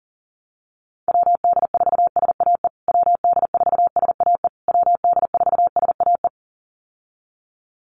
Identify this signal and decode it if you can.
Morse code, heard as a keyed tone, about 40 wpm, 700 Hz. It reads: WD4SAE WD4SAE WD4SAE
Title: USA Call Signs Repeated 3x 40wpm